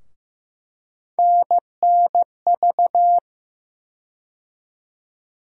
Morse code, heard as a keyed tone, about 15 wpm, 700 Hz. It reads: NNV
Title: USA Call Sign Postfixes - 3 Letters 15wpm